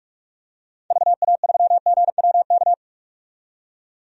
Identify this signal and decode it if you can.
Morse code, heard as a keyed tone, about 46 wpm, 700 Hz. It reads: VA3GWK
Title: DXCC-100 Call Signs - 6 Characters - 45wpm